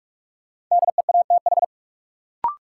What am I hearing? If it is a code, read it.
Morse code, heard as a keyed tone, about 46 wpm, 700 Hz. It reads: DEATH
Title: Sentences from Top 1100 Words - New Word Drill Alt - 45wpm